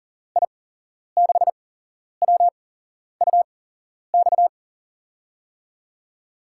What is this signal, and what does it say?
Morse code, heard as a keyed tone, about 37 wpm, spaced as for 11 wpm, 700 Hz. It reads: I6WUX
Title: USA Call Signs - 6 Characters - 8x Character Spacing - 40wpm